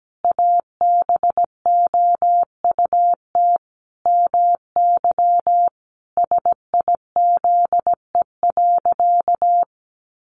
Morse code, heard as a keyed tone, about 17 wpm, 700 Hz. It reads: ABOUT MY SIZE.
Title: Sentences from Top 400 Words 17wpm